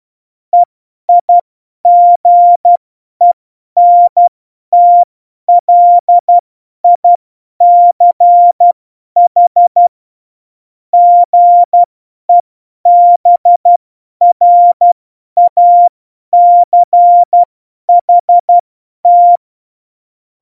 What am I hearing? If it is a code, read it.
Morse code, heard as a keyed tone, about 12 wpm, spaced as for 10 wpm, 700 Hz. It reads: EIGENTLICH GEBRACHT
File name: Morsekurs / Witze / Witz 03 April 2018, 50 F70